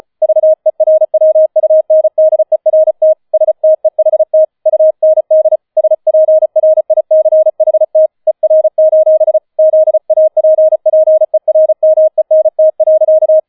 Morse code, heard as a keyed tone, about 35 wpm, 630 Hz. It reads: VERWUNDERT STEHT UND SPRICHT ER: ZAPPERMENT.